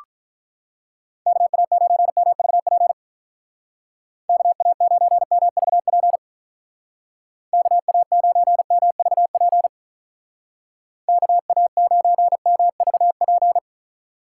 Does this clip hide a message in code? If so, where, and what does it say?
Yes: Morse code from 1.3 to 13.6 s, heard as a keyed tone, about 46 wpm, 700 Hz. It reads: KA9MVP KA9MVP KA9MVP KA9MVP